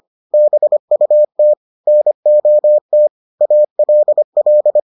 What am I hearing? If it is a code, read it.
Morse code, heard as a keyed tone, about 25 wpm, 600 Hz. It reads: BUT NOT ALL